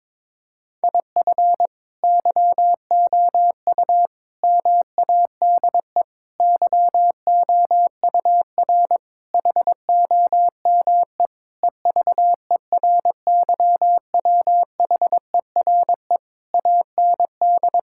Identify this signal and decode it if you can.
Morse code, heard as a keyed tone, about 22 wpm, 700 Hz. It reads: IF YOU MADE YOUR HOME EVERYWHERE AND
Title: Sentences from Top 1600 Words 22wpm